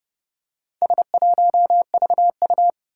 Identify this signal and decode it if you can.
Morse code, heard as a keyed tone, about 30 wpm, 700 Hz. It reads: S1VU
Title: Four Challenging Letter-Number Combinations 30wpm